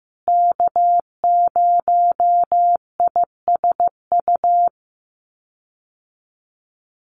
Morse code, heard as a keyed tone, about 15 wpm, 700 Hz. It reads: K0ISU
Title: USA Call Signs Repeated 2x 15wpm